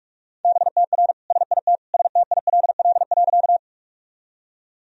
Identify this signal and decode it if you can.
Morse code, heard as a keyed tone, about 43 wpm, 700 Hz. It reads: BTR SIT STILL.
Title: Sentences from Top 500 Words and Abbreviations 45wpm